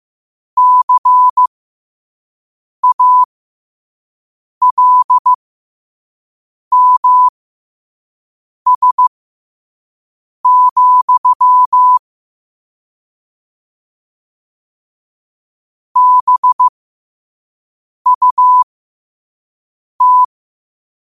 Morse code, heard as a keyed tone, about 15 wpm, spaced as for 5 wpm, 1000 Hz. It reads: CALMS, BUT
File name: 5wpm